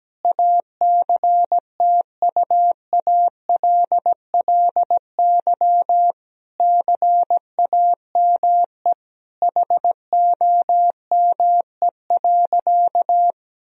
Morse code, heard as a keyed tone, about 17 wpm, 700 Hz. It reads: ACTUALLY CAME HOME.
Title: Sentences from Top 1000 Words - Review Part 4 - 17wpm